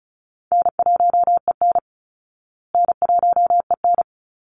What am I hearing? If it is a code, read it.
Morse code, heard as a keyed tone, about 34 wpm, 700 Hz. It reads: N1EN N1EN
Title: Call Signs Repeated 2x - 35wpm